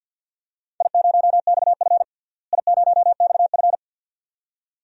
Morse code, heard as a keyed tone, about 52 wpm, 700 Hz. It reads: I0XF I0XF
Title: DXCC-100 Call Signs - 4 Characters - Repeated 2x - 50wpm